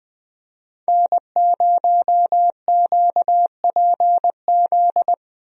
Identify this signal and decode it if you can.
Morse code, heard as a keyed tone, about 20 wpm, 700 Hz. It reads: N0QPZ
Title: USA Call Signs - 5 Characters - Repeated 3x - 20wpm